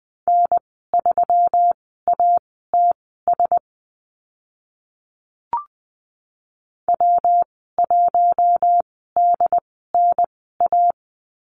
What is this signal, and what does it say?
Morse code, heard as a keyed tone, about 20 wpm, spaced as for 14 wpm, 700 Hz. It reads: N3ATS W1DNA
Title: 122 Lesson 20g - Number 3 - Call Signs 20wpm